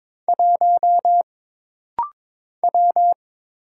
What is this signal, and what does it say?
Morse code, heard as a keyed tone, about 22 wpm, 700 Hz. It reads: 1 W
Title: Single Letter-Number - Rapid-Fire 22wpm